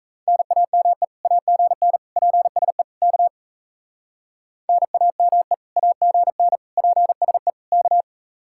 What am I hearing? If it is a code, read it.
Morse code, heard as a keyed tone, about 33 wpm, 700 Hz. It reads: NAME AGN PSE K NAME AGN PSE K
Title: Speed-Racing Common QSO Phrases - 28wpm